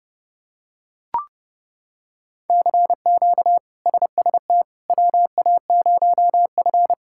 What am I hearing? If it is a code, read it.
Morse code, heard as a keyed tone, about 32 wpm, 700 Hz. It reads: CQ SST WA0F
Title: K1USN SST QSO Practice - 30wpm